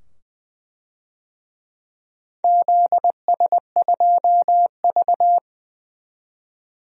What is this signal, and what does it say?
Morse code, heard as a keyed tone, about 21 wpm, 700 Hz. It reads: ZS2V
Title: DXCC-100 Call Signs - 4 Characters - 20wpm